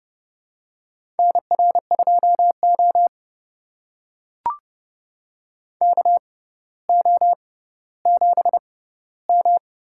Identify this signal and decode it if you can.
Morse code, heard as a keyed tone, about 30 wpm, spaced as for 15 wpm, 700 Hz. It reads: NR2O KO7M
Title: USA Call Signs - 4 Characters - 6x Character Spacing - 30wpm